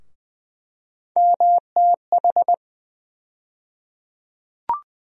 Morse code, heard as a keyed tone, about 20 wpm, 700 Hz. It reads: MTH
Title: USA Call Sign Postfixes - 3 Letters 20wpm